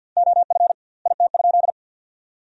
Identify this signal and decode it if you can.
Morse code, heard as a keyed tone, about 52 wpm, 700 Hz. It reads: OF IT?
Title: 2x Word-Spacing - Sentences from Top 100 Words - 50wpm